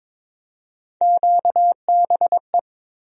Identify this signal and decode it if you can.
Morse code, heard as a keyed tone, about 22 wpm, 700 Hz. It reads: QBE
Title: USA Call Sign Postfixes - 3 Letters 22wpm